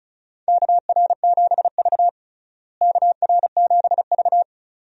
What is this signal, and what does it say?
Morse code, heard as a keyed tone, about 36 wpm, 700 Hz. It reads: KR7V KR7V
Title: USA Call Signs - 4 Characters - Repeated 2x - 35wpm